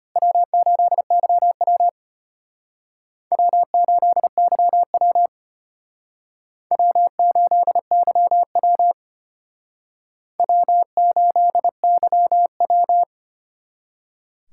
Morse code, heard as a keyed tone, about 32 wpm, 700 Hz. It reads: W8YW W8YW W8YW W8YW